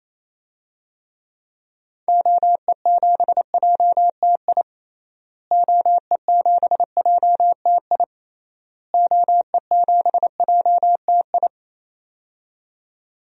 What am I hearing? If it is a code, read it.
Morse code, heard as a keyed tone, about 27 wpm, 700 Hz. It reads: OE7JTI OE7JTI OE7JTI
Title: DXCC-100 Call Signs - 6 Characters - Repeated 3x - 28wpm